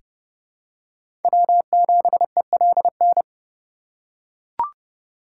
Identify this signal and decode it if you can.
Morse code, heard as a keyed tone, about 30 wpm, 700 Hz. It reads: W7ELN
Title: USA Call Signs - 4x Character Spacing - 30wpm